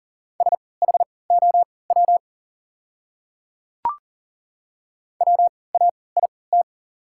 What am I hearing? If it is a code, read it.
Morse code, heard as a keyed tone, about 40 wpm, spaced as for 23 wpm, 700 Hz. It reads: SHOW WAIT
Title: 3x Character Spacing - Top 500 Words - 40wpm